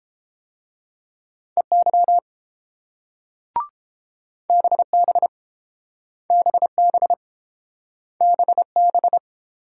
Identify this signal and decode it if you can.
Morse code, heard as a keyed tone, about 30 wpm, 700 Hz. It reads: EY BB BB BB